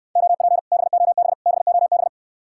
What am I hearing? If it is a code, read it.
Morse code, heard as a keyed tone, about 69 wpm, 700 Hz. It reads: OP BOB BOB